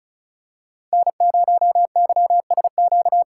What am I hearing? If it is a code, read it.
Morse code, heard as a keyed tone, about 36 wpm, 700 Hz. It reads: N0YSQ
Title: K1USN SST QSO Practice - 35wpm